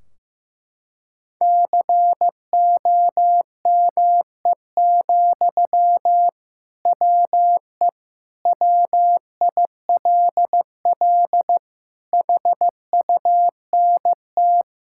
Morse code, heard as a keyed tone, about 15 wpm, 700 Hz. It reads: COME, WE WILL HUNT
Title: Sentences from Top 1600 Words 15wpm